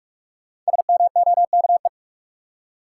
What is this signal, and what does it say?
Morse code, heard as a keyed tone, about 47 wpm, 700 Hz. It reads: SMOKE
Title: Five Letter Words - 45wpm